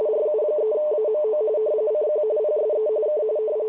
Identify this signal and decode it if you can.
RTTY, about 45.45 baud. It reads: RU3AMO CQ PSE K
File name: luxorion / Radio / qso-rtty45-cq-ru3amo- -1723z14084-7khz